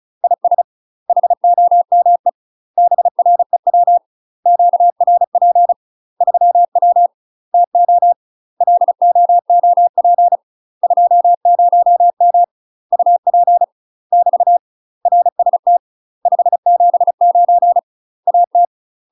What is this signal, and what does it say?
Morse code, heard as a keyed tone, about 36 wpm, 700 Hz. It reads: IS HOME BREW QRP 3W TO LOOP 20M UP <BT> RST 579 AT
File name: UK MASTER QSOs TRAINING / 35WPM / UK MASTER QSOs 0018